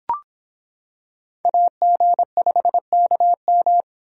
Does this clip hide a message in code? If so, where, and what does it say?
Yes: Morse code from 1.4 to 3.8 s, heard as a keyed tone, about 25 wpm, 700 Hz. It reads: AG5KM